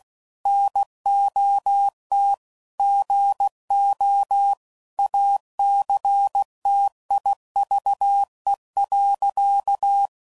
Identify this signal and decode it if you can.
Morse code, heard as a keyed tone, about 16 wpm, 800 Hz. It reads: NOT GO ACTIVE.
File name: CW Code Groups @15WPM / Technical Subjects / OR Gate